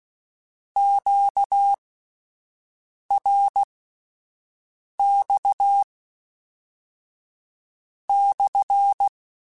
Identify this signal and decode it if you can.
Morse code, heard as a keyed tone, about 16 wpm, spaced as for 5 wpm, 800 Hz. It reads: QRX/
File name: Q Calls / Q Calls-mp3s / Q Calls SPSL1 @10